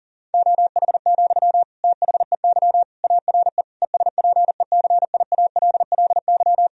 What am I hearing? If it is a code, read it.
Morse code, heard as a keyed tone, about 40 wpm, 700 Hz. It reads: OH, THEY ARE ESPECIALLY
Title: Sentences from Top 1000 Words - Review Part 4 - 40wpm